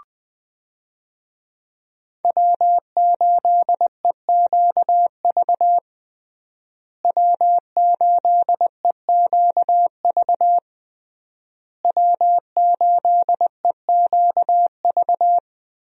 Morse code, heard as a keyed tone, about 20 wpm, 700 Hz. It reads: W8EQV W8EQV W8EQV
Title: USA Call Signs - 5 Characters - Repeated 3x - 20wpm